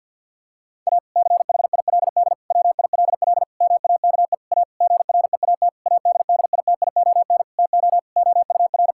Morse code, heard as a keyed tone, about 50 wpm, 700 Hz. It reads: A CHILD WILL MAKE A GREAT ADDITION TO OUR
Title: Sentences from Top 1000 Words - Review Part 6 - 50wpm